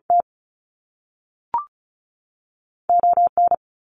Morse code, heard as a keyed tone, about 36 wpm, 700 Hz. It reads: T O N
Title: Canadian Provinces and Territories 35wpm